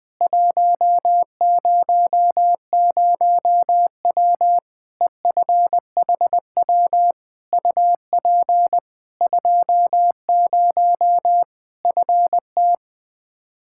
Morse code, heard as a keyed tone, about 20 wpm, 700 Hz. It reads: 100W EFHW UP 20 FT
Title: Practice QSO - Exchange 2 - Info - 20wpm